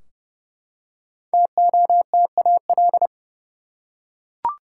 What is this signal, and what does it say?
Morse code, heard as a keyed tone, about 32 wpm, 700 Hz. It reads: TOTAL
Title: Five Letter Words - 30wpm